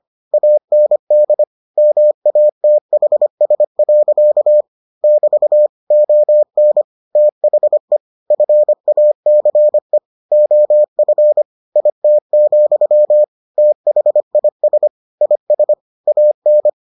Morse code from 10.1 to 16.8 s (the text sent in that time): OF IT, THIS IS AN